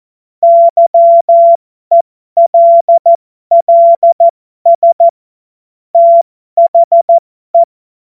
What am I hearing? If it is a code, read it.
Morse code, heard as a keyed tone, about 14 wpm, spaced as for 12 wpm, 680 Hz. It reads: YELLS THE
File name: Morsekurs / Witze